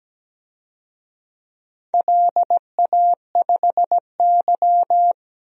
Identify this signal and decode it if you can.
Morse code, heard as a keyed tone, about 17 wpm, 700 Hz. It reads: LA5Y
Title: DXCC-100 Call Signs - 4 Characters - Repeated 3x - 17wpm